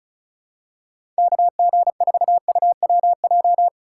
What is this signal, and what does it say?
Morse code, heard as a keyed tone, about 36 wpm, 700 Hz. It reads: KG4UWJ
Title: USA Call Signs - 6 Characters - 2x Character Spacing - 35wpm